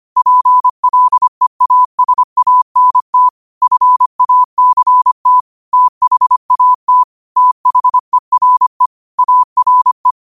Morse code, heard as a keyed tone, about 25 wpm, 1000 Hz. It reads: PLEASANT FACT THAT THERE ARE
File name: qotdfiles / 25wpm